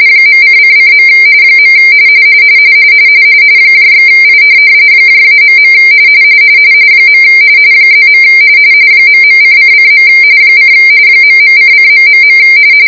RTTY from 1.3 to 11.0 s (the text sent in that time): . /I -.. +. ..:::. :